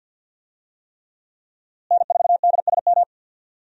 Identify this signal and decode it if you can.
Morse code, heard as a keyed tone, about 50 wpm, 700 Hz. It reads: N4DSM
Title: USA Call Signs - 5 Characters - Repeated 3x - 50wpm